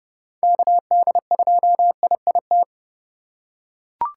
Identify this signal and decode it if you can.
Morse code, heard as a keyed tone, about 30 wpm, 700 Hz. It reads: KD2IIT